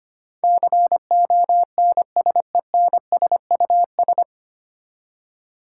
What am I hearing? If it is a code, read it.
Morse code, heard as a keyed tone, about 25 wpm, 700 Hz. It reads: CONSENSUS
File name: Nine Letter Words - 25wpm